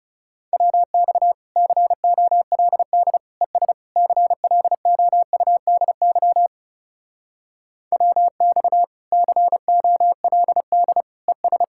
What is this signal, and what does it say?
Morse code, heard as a keyed tone, about 38 wpm, 700 Hz. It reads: WX COLD ES CLOUDY WX COLD ES